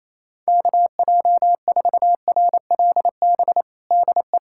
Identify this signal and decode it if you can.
Morse code, heard as a keyed tone, about 28 wpm, 700 Hz. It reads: KJ4RLB DE
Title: USA Callsign DE Callsign - 6 Characters - 28wpm